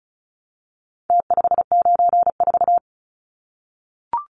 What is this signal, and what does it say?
Morse code, heard as a keyed tone, about 34 wpm, 700 Hz. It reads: T594
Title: Numbers - Contest Style - 4-Digit 35wpm